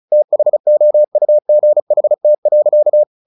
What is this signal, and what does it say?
Morse code, heard as a keyed tone, about 34 wpm, 600 Hz. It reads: THOUGHT.
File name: the old man and the sea 2 35wpm